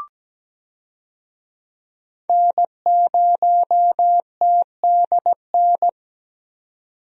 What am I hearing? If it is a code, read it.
Morse code, heard as a keyed tone, about 17 wpm, 700 Hz. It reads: N0TDN
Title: USA Call Signs - 5 Characters - 17wpm